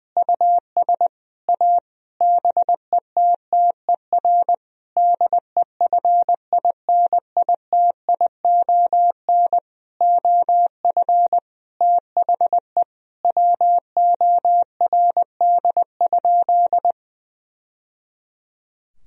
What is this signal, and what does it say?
Morse code, heard as a keyed tone, about 20 wpm, 700 Hz. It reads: US A BETTER DEFINITION OF THE WORD?
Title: Sentences from Top 1000 Words - Review Part 4 - 20wpm